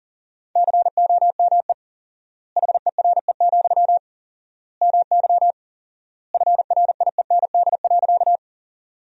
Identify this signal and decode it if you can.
Morse code, heard as a keyed tone, about 37 wpm, 700 Hz. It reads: COME HERE, MY FRIEND.
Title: 4x Word-Spacing - Sentences from Top 300 Words - 40wpm